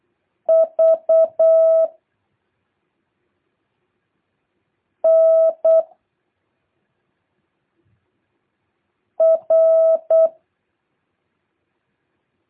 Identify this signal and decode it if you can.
Morse code, heard as a keyed tone, about 8 wpm, 660 Hz. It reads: V N R
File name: jh8cbh / 2023cw